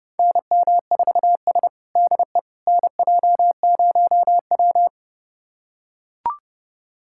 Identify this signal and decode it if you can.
Morse code, heard as a keyed tone, about 30 wpm, 700 Hz. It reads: NM4S DE NJ0W
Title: USA Callsign DE Callsign - 4 Characters - 30wpm